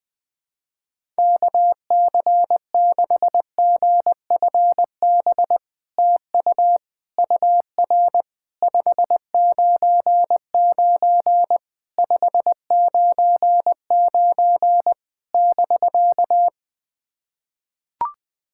Morse code, heard as a keyed tone, about 20 wpm, 700 Hz. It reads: KC6GFB TU UR 599 599 <BK>
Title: SOTA QSO Practice 20wpm